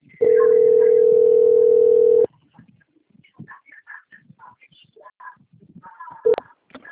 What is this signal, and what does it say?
Ringback tone (North America), 440 + 480 Hz.
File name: out- - - - - .40738